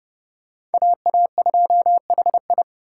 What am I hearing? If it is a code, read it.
Morse code, heard as a keyed tone, about 26 wpm, 700 Hz. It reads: AA2HI